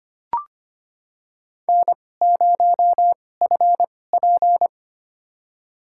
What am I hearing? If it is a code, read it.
Morse code, heard as a keyed tone, about 25 wpm, spaced as for 18 wpm, 700 Hz. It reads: N0FP
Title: USA Call Signs - 4 Characters - 2x Character Spacing - 25wpm